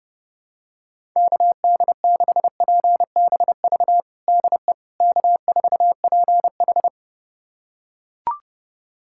Morse code, heard as a keyed tone, about 31 wpm, 700 Hz. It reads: KD6PBV DE K4PH
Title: USA Callsign DE Callsign 30wpm